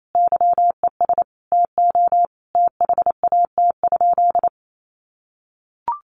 Morse code, heard as a keyed tone, about 28 wpm, 700 Hz. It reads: YES TO THAT?
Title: Sentences from Top 1100 Words 28wpm